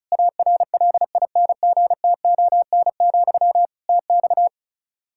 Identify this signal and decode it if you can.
Morse code, heard as a keyed tone, about 36 wpm, 700 Hz. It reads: ARLINGTON, TX
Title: Practice QSO - Exchange 1 - Beginning - 35wpm